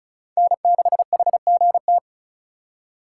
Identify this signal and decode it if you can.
Morse code, heard as a keyed tone, about 36 wpm, 700 Hz. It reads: N6HGT
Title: USA Call Signs - 4x Character Spacing - 35wpm